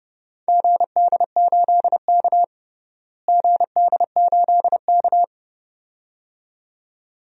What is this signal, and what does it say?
Morse code, heard as a keyed tone, about 30 wpm, 700 Hz. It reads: GD8K GD8K
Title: DXCC-100 Call Signs - 4 Characters - Repeated 2x - 30wpm